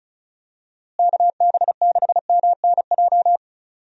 Morse code, heard as a keyed tone, about 36 wpm, 700 Hz. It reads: KB6MNJ
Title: USA Call Signs - 6 Characters - Repeated 2x - 35wpm